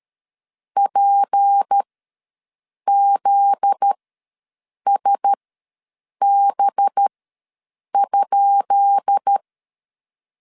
Morse code, heard as a keyed tone, about 13 wpm, spaced as for 7 wpm, 800 Hz. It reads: PZSB?
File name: LEZIONI / 10 SIGNS / 10D COMA ws 12wpm 3618